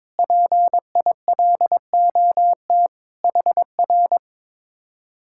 Morse code, heard as a keyed tone, about 22 wpm, 700 Hz. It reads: PILOT HR
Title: Speed-Racing Common QSO Phrases Encore - 22wpm